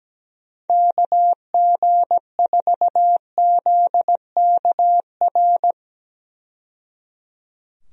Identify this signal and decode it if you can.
Morse code, heard as a keyed tone, about 17 wpm, 700 Hz. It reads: KG4ZKR
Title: USA Call Signs - 6 Characters - 17wpm